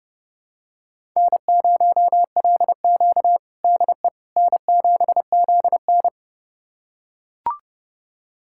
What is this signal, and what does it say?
Morse code, heard as a keyed tone, about 30 wpm, 700 Hz. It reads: N0LQ DE N7ZN
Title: USA Callsign DE Callsign - 4 Characters - 30wpm